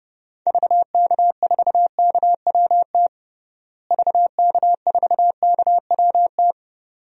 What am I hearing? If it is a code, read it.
Morse code, heard as a keyed tone, about 30 wpm, 700 Hz. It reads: VK4KWT VK4KWT
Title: DXCC-100 Call Signs - 6 Characters - Repeated 2x - 30wpm